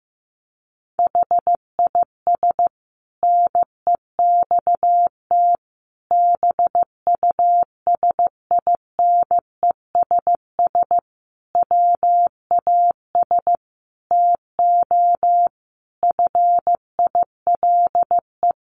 Morse code, heard as a keyed tone, about 15 wpm, 700 Hz. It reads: HIS NEXT BUSINESS WAS TO FILE
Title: Sentences from Top 1300 Words 15wpm